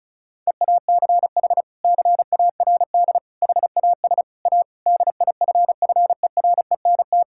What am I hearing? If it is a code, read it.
Morse code, heard as a keyed tone, about 34 wpm, 700 Hz. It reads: EACH CARD HAS A DIFFERENT
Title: Sentences from Top 1000 Words - Review Part 6 - 35wpm